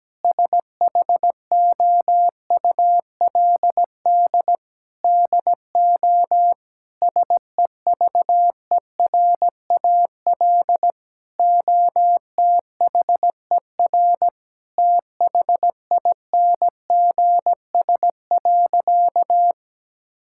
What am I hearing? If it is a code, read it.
Morse code, heard as a keyed tone, about 17 wpm, 700 Hz. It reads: SHOULD DO SEVERAL OTHER THINGS.
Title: Sentences from Top 1000 Words - Review Part 6 - 17wpm